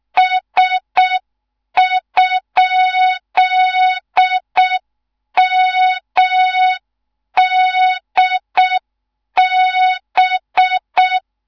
Morse code, heard as a keyed tone, about 6 wpm, 3700 Hz. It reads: S?MDB
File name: Morze-HA2MN / hang / 14 280 HA2MN